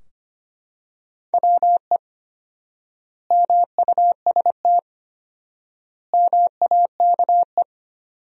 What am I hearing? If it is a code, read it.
Morse code, heard as a keyed tone, about 25 wpm, 700 Hz. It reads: WE MUST MAKE
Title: 4x Word-Spacing - Sentences from Top 400 Words - 25wpm